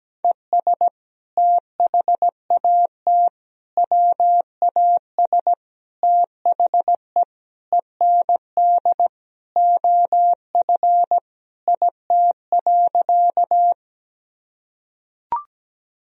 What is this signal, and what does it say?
Morse code, heard as a keyed tone, about 17 wpm, 700 Hz. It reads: ES THAT WAS THE END OF IT.
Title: Sentences from Top 200 Words and Abbreviations 17wpm